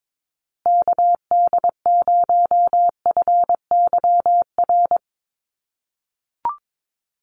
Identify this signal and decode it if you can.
Morse code, heard as a keyed tone, about 22 wpm, 700 Hz. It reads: KD0FYR
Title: USA Call Signs Repeated 2x 22wpm